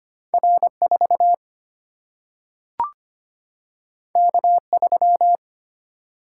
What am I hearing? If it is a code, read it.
Morse code, heard as a keyed tone, about 25 wpm, 700 Hz. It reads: R 4 K 3
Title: Single-Letter Single-Number 25wpm